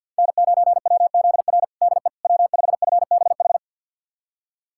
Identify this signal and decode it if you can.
Morse code, heard as a keyed tone, about 50 wpm, 700 Hz. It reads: N9WZR DE W5FBH
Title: USA Callsign DE Callsign - 5 Characters - 50wpm